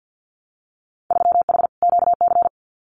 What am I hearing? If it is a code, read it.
Morse code, heard as a keyed tone, about 50 wpm, 700 Hz. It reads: 3H QC
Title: Winter Field Day Exchanges Encore 50wpm